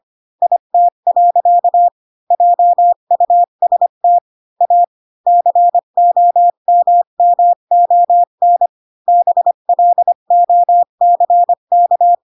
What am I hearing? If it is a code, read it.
Morse code, heard as a keyed tone, about 25 wpm, 700 Hz. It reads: IT. JUST A COMMON BLOCK
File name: Pinocchio / Pinocchio ch1 20 25 ik1hge